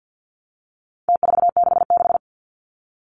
Morse code, heard as a keyed tone, about 49 wpm, 700 Hz. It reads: T466
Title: Numbers - Contest Style - 4-Digit 50wpm